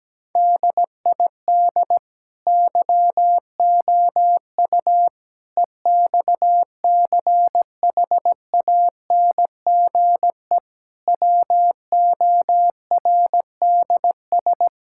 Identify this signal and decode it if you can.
Morse code, heard as a keyed tone, about 17 wpm, 700 Hz. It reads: DID YOU EXCHANGE WORDS
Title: Sentences from Top 1000 Words - Review Part 6 - 17wpm